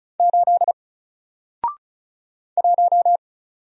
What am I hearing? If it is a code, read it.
Morse code, heard as a keyed tone, about 34 wpm, 700 Hz. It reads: 8 1
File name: Single-Letter-Number-35wpm